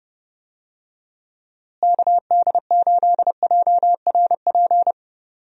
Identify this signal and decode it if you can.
Morse code, heard as a keyed tone, about 30 wpm, 700 Hz. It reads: KD8JRP
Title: USA Call Signs - 6 Characters - Repeated 3x - 30wpm